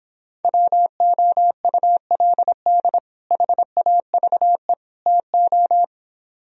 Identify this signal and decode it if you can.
Morse code, heard as a keyed tone, about 26 wpm, 700 Hz. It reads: WOULD HAVE TO